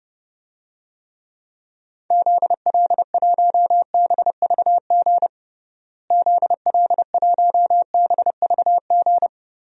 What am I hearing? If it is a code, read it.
Morse code, heard as a keyed tone, about 30 wpm, 700 Hz. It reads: ZL1BVG ZL1BVG
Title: DXCC-100 Call Signs - 6 Characters - Repeated 3x - 30wpm